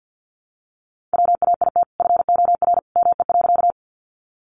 Morse code, heard as a keyed tone, about 50 wpm, 700 Hz. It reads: WAIT FOR ME.
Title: Sentences from Top 500 Words 50wpm